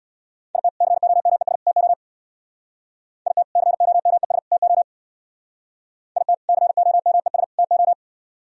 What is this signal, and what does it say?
Morse code, heard as a keyed tone, about 64 wpm, 700 Hz. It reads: IT COMES TO IT COMES TO IT COMES TO